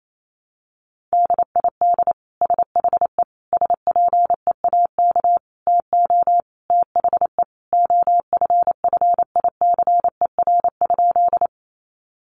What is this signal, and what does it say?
Morse code, heard as a keyed tone, about 28 wpm, 700 Hz. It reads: DID SHE SPEAK TO THE OFFICER?
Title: Sentences from Top 1200 Words 28wpm